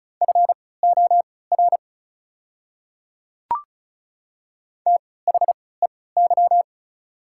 Morse code, heard as a keyed tone, about 34 wpm, spaced as for 20 wpm, 690 Hz. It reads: FOR THEY
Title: 3x Character Spacing - Top 100 Words - 35wpm